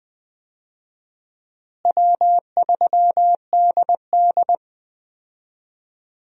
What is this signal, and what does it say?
Morse code, heard as a keyed tone, about 20 wpm, 700 Hz. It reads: W3DD
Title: USA Call Signs - 4 Characters - Repeated 2x - 20wpm